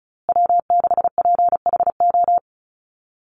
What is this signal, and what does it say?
Morse code, heard as a keyed tone, about 34 wpm, 700 Hz. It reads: W6PHO
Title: Call Signs Repeated 2x - 35wpm